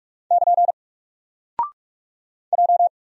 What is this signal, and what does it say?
Morse code, heard as a keyed tone, about 44 wpm, 700 Hz. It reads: <KN> J
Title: Single Character (Letter:Number:Punctuation:Prosign) 45wpm